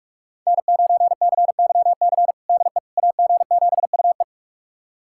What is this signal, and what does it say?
Morse code, heard as a keyed tone, about 41 wpm, 700 Hz. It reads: N9CYC DE AG7UE
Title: USA Callsign DE Callsign - 5 Characters - 45wpm